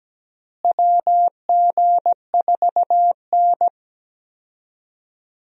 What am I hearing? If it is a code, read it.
Morse code, heard as a keyed tone, about 17 wpm, 700 Hz. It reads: WG4N
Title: USA Call Signs - 4 Characters - Repeated 3x - 17wpm